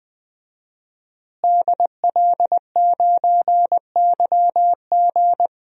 Morse code, heard as a keyed tone, about 20 wpm, 700 Hz. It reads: DL9YG
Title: DXCC-100 Call Signs - 5 Characters - Repeated 2x - 20wpm